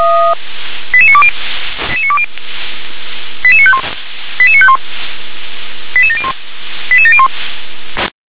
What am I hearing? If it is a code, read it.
Five-tone selective call (ZVEI), heard as quick runs of five tones, tones of about 70 ms. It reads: 88920 88951 88951 88701 88701